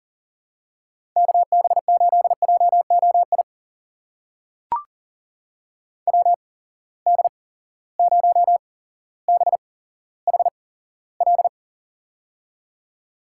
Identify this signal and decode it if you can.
Morse code, heard as a keyed tone, about 39 wpm, spaced as for 17 wpm, 700 Hz. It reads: KB8JOI WD0BHL
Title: USA Call Signs - 6 Characters - 8x Character Spacing - 40wpm